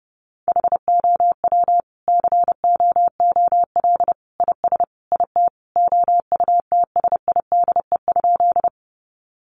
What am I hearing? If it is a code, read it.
Morse code, heard as a keyed tone, about 30 wpm, 700 Hz. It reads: HOW COOL IS IT OUTSIDE?
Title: Sentences from Top 1200 Words 30wpm